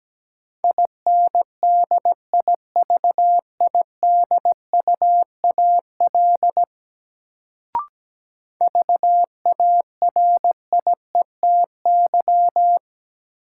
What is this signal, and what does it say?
Morse code, heard as a keyed tone, about 17 wpm, 700 Hz. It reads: INDIVIDUAL VARIETY
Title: Sentences from Top 1000 Words - Review Part 4 - New Word Drill - 17wpm